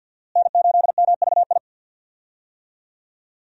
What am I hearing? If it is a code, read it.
Morse code, heard as a keyed tone, about 52 wpm, 700 Hz. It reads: N8MVI
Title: USA Call Signs - 5 Characters - Repeated 2x - 50wpm